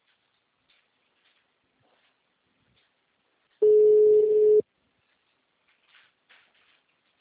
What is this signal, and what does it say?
Ringback tone (Europe, 425 Hz).
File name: out- -2024- - - .40